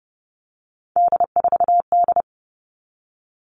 Morse code, heard as a keyed tone, about 30 wpm, 700 Hz. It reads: D4D
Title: All Cut Numbers - 3-Digit 30wpm